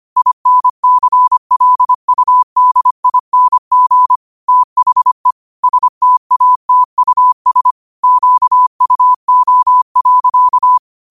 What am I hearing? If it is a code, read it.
Morse code, heard as a keyed tone, about 25 wpm, 1000 Hz. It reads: INCLUDING THE STATUS QUO.
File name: 25wpm